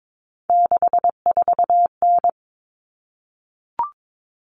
Morse code, heard as a keyed tone, about 22 wpm, 700 Hz. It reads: 64N
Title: Common Cut Numbers - 3-Digit 22wpm